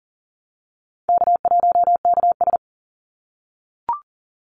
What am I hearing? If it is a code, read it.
Morse code, heard as a keyed tone, about 40 wpm, 700 Hz. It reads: K1KS
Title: Easy Call Signs - 40wpm